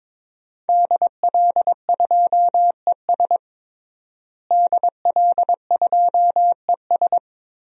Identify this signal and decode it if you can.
Morse code, heard as a keyed tone, about 22 wpm, 700 Hz. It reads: DL2ES DL2ES
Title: DXCC-100 Call Signs - 5 Characters - Repeated 2x - 22wpm